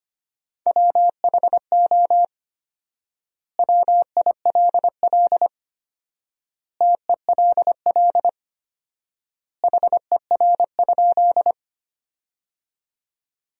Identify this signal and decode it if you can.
Morse code, heard as a keyed tone, about 25 wpm, 700 Hz. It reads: WHO WILL TELL HER?
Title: 4x Word-Spacing - Sentences from Top 200 Words - 25wpm